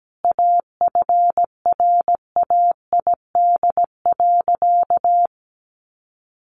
Morse code, heard as a keyed tone, about 17 wpm, 700 Hz. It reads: AFRAID.
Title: Sentences from Top 1500 Words 17wpm